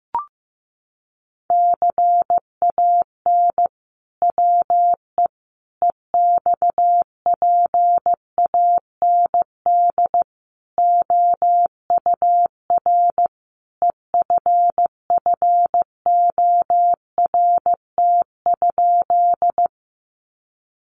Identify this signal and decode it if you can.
Morse code, heard as a keyed tone, about 15 wpm, 700 Hz. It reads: CAN WE EXPAND OUR EFFORT?
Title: Sentences from Top 1400 Words 15wpm